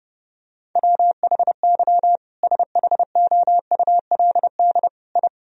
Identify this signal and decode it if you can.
Morse code, heard as a keyed tone, about 30 wpm, 700 Hz. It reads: WHY SHOULD I
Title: Sentences from Top 1000 Words - Review Part 3 - 30wpm